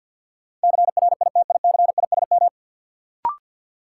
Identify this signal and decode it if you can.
Morse code, heard as a keyed tone, about 53 wpm, 700 Hz. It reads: CRITICISM
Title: Sentences from Top 1300 Words - New Word Drill - 50wpm